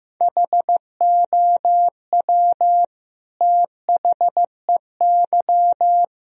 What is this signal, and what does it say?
Morse code, heard as a keyed tone, about 15 wpm, 700 Hz. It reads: HOW THEY
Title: Sentences from Top 1000 Words - Review Part 4 - 15wpm